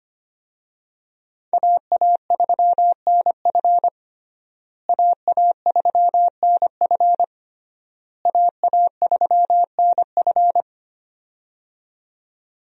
Morse code, heard as a keyed tone, about 25 wpm, 700 Hz. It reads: AA3NF AA3NF AA3NF
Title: USA Call Signs - 5 Characters - Repeated 3x - 25wpm